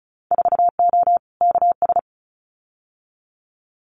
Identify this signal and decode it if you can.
Morse code, heard as a keyed tone, about 36 wpm, 700 Hz. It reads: 4O KS
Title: Winter Field Day Exchanges Encore 35wpm